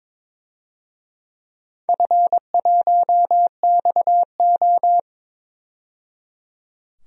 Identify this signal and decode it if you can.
Morse code, heard as a keyed tone, about 22 wpm, 700 Hz. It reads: F1XO
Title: DXCC-100 Call Signs - 4 Characters - 22wpm